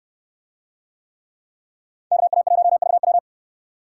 Morse code, heard as a keyed tone, about 66 wpm, 700 Hz. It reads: KA1VW